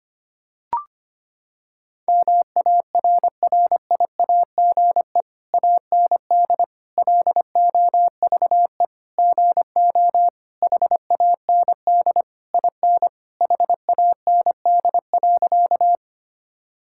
Morse code, heard as a keyed tone, about 25 wpm, 700 Hz. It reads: MARRIAGE AND LOVE GO HAND IN HAND.
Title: Sentences from Top 1000 Words - Review Part 5 - 25wpm